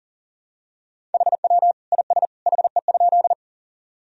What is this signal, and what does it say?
Morse code, heard as a keyed tone, about 37 wpm, 700 Hz. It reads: HW IS HE?
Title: Sentences from Top 100 Words and Abbreviations 40wpm